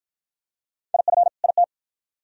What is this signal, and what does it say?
Morse code, heard as a keyed tone, about 50 wpm, 700 Hz. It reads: IF IT